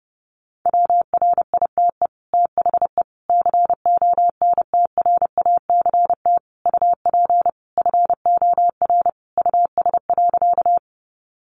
Morse code, heard as a keyed tone, about 30 wpm, 700 Hz. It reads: WRITE THE CONTRACT UP FOR US.
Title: Sentences from Top 1400 Words 30wpm